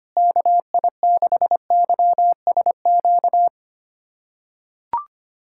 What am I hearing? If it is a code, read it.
Morse code, heard as a keyed tone, about 25 wpm, 700 Hz. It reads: KI6YSQ
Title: USA Callsign DE Callsign - 6 Characters - 25wpm